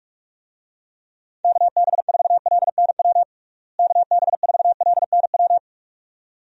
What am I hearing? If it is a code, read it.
Morse code, heard as a keyed tone, about 46 wpm, 700 Hz. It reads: KB4LNW KB4LNW
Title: USA Call Signs - 6 Characters - Repeated 2x - 45wpm